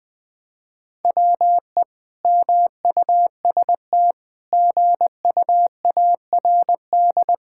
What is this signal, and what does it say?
Morse code, heard as a keyed tone, about 21 wpm, 700 Hz. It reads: WE MUST GUARD
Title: Sentences from Top 1600 Words 20wpm